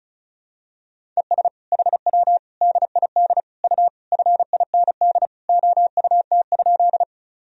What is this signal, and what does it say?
Morse code, heard as a keyed tone, about 34 wpm, 700 Hz. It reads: ES HW DID U FIND OUT?
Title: Sentences from Top 100 Words and Abbreviations 35wpm